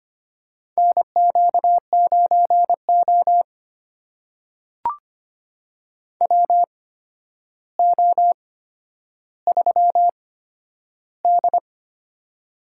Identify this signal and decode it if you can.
Morse code, heard as a keyed tone, about 25 wpm, spaced as for 11 wpm, 700 Hz. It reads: NQ9O WO3D
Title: USA Call Signs - 4 Characters - 8x Character Spacing - 25wpm